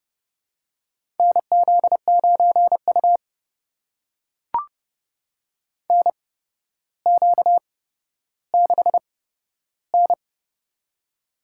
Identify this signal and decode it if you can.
Morse code, heard as a keyed tone, about 30 wpm, spaced as for 13 wpm, 700 Hz. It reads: NZ9U NQ6N
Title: USA Call Signs - 8x Character Spacing - 30wpm